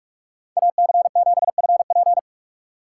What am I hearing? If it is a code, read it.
Morse code, heard as a keyed tone, about 45 wpm, 700 Hz. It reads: AC7FP
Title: USA Call Signs - 5 Characters - 6x Character Spacing - 45wpm